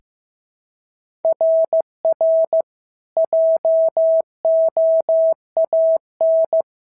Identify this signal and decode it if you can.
Morse code, heard as a keyed tone, about 14 wpm, 650 Hz. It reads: RR JOAN
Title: SOTA QSO Practice 15wpm